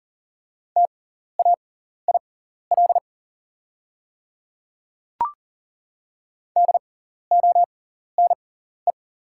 Morse code, heard as a keyed tone, about 37 wpm, spaced as for 14 wpm, 700 Hz. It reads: TAIL DONE
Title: 6x Character Spacing - Top 500 Words - 40wpm